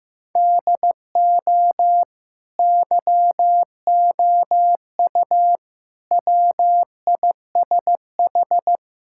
Morse code, heard as a keyed tone, about 15 wpm, 700 Hz. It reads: DO YOU WISH
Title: Sentences from Top 1600 Words 15wpm